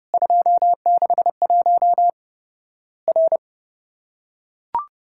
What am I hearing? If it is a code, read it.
Morse code, heard as a keyed tone, about 30 wpm, 700 Hz. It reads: 261 R
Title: Worked All Europe DX Contest - QTC Exchange - 30wpm